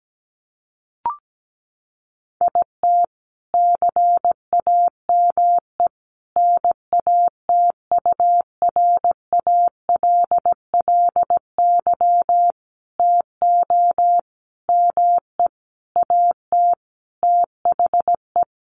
Morse code, heard as a keyed tone, about 17 wpm, 700 Hz. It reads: IT CAME NATURALLY TO ME AT THE
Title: Sentences from Top 1300 Words 17wpm